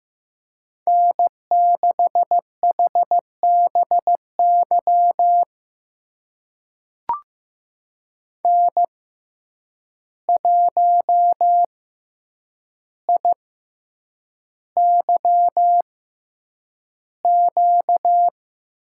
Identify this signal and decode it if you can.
Morse code, heard as a keyed tone, about 16 wpm, spaced as for 8 wpm, 700 Hz. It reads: N6HBY N1IYQ
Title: USA Call Signs - 6x Character Spacing - 15wpm